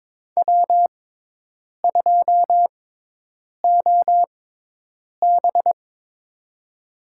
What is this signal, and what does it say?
Morse code, heard as a keyed tone, about 22 wpm, spaced as for 8 wpm, 700 Hz. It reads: W2OB
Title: USA Call Signs - 4 Characters - 6x Character Spacing - 22wpm